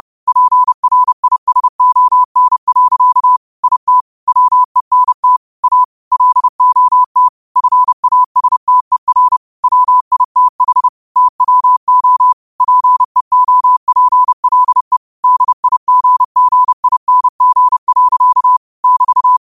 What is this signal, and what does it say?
Morse code, heard as a keyed tone, about 30 wpm, 1000 Hz. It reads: PRISON. IT WENT A LOT FASTER WITH TWO PEOPLE DIGGING. <BT>
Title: 30 WPM morse code quotes for Thu, 29 Feb 2024